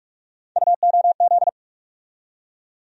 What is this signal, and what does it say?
Morse code, heard as a keyed tone, about 47 wpm, 700 Hz. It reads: UOZ
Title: USA Call Sign Postfixes - 3 Letters 45wpm